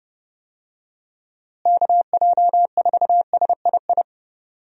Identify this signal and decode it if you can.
Morse code, heard as a keyed tone, about 30 wpm, 700 Hz. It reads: KJ4SII
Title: USA Call Signs - 6 Characters - Repeated 2x - 30wpm